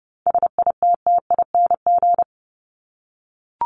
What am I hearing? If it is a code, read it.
Morse code, heard as a keyed tone, about 30 wpm, 700 Hz. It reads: SITTING
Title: ING Suffix - Top 100 Words with Base Words 30wpm